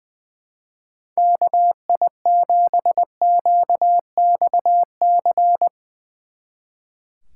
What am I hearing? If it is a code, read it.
Morse code, heard as a keyed tone, about 21 wpm, 700 Hz. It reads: KI7QXC
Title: Speed Racing - USA 2x3 Call Signs 20wpm